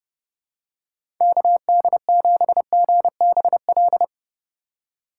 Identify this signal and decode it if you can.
Morse code, heard as a keyed tone, about 30 wpm, 700 Hz. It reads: KD7GBL
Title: USA Call Signs - 6 Characters - 2x Character Spacing - 30wpm